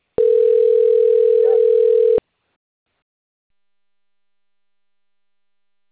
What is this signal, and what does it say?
Ringback tone (North America), 440 + 480 Hz.
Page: Atibaia 1 CP Cartao